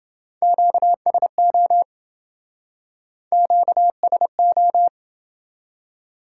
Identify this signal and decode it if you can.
Morse code, heard as a keyed tone, about 28 wpm, 700 Hz. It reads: QSO QSO